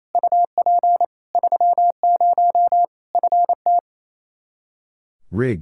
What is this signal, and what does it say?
Morse code, heard as a keyed tone, about 28 wpm, 700 Hz. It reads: UP 30 FT
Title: Practice QSO - Exchange 2 - Info - 28wpm